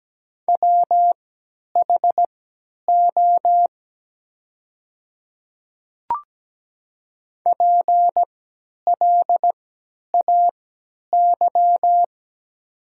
Morse code, heard as a keyed tone, about 17 wpm, spaced as for 10 wpm, 700 Hz. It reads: WHO PLAY
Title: 3x Character Spacing - Top 200 Words - 17wpm